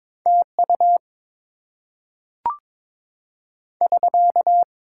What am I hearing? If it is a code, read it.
Morse code, heard as a keyed tone, about 22 wpm, 700 Hz. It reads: TU <SK>
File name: Top 90 QSO Elements - 6x Character Spacing - 22wpm